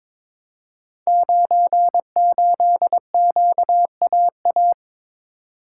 Morse code, heard as a keyed tone, about 22 wpm, 700 Hz. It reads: 98QAA
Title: Code Groups - 5 Letter:Numbers - 22wpm